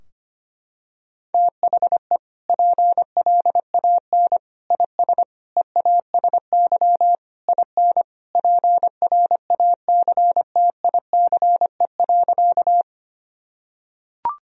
Morse code, heard as a keyed tone, about 25 wpm, 690 Hz. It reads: THE PLAN IS EASY IN PRACTICE.
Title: Sentences from Top 1000 Words - Review Part 4 - 25wpm